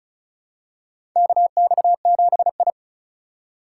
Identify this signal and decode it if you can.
Morse code, heard as a keyed tone, about 34 wpm, 700 Hz. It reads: KX7I
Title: USA Call Signs - 4 Characters - 2x Character Spacing - 35wpm